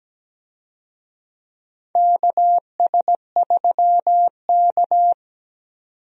Morse code, heard as a keyed tone, about 17 wpm, 700 Hz. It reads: KS3K
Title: USA Call Signs - 4 Characters - 17wpm